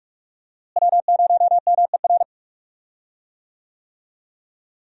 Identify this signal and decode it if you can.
Morse code, heard as a keyed tone, about 46 wpm, 700 Hz. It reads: W0MER
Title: USA Call Signs - 5 Characters - Repeated 3x - 45wpm